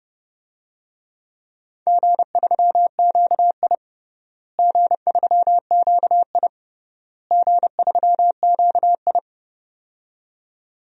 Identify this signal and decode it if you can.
Morse code, heard as a keyed tone, about 30 wpm, 700 Hz. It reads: G3QI G3QI G3QI
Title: DXCC-100 Call Signs - 4 Characters - Repeated 3x - 30wpm